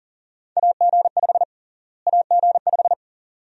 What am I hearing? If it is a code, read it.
Morse code, heard as a keyed tone, about 40 wpm, 700 Hz. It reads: AG5 AG5
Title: USA Call Sign Prefixes - Repeated 2x - 40wpm